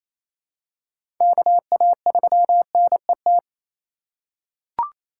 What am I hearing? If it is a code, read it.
Morse code, heard as a keyed tone, about 28 wpm, 700 Hz. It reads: KA3NET
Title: USA Call Signs - 6 Characters - 3x Character Spacing - 28wpm